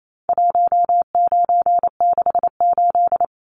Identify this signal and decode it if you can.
Morse code, heard as a keyed tone, about 28 wpm, 700 Hz. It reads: 1968
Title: Numbers - 4-Digit 28wpm